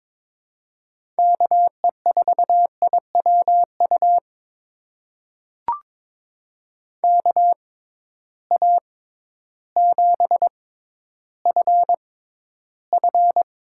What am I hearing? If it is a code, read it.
Morse code, heard as a keyed tone, about 22 wpm, spaced as for 12 wpm, 700 Hz. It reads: KE4IWU KA7FF
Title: USA Call Signs - 6x Character Spacing - 22wpm